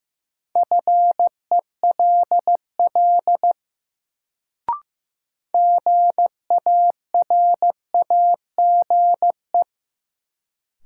Morse code, heard as a keyed tone, about 15 wpm, 700 Hz. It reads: FELL GARAGE
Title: Sentences from Top 1600 Words - New Word Drill - 15wpm